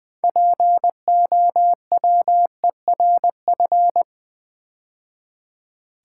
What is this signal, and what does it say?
Morse code, heard as a keyed tone, about 20 wpm, 700 Hz. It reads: POWERF
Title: Build A Word - 2 Letters At A Time - 8-Letter Words - 20wpm